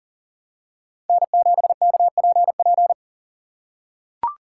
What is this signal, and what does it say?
Morse code, heard as a keyed tone, about 40 wpm, 700 Hz. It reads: N7KPP
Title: USA Call Signs - 5 Characters - 3x Character Spacing - 40wpm